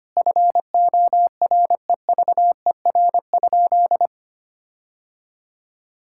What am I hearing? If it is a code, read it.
Morse code, heard as a keyed tone, about 25 wpm, 700 Hz. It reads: FOREVER?
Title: Sentences from Top 1600 Words 25wpm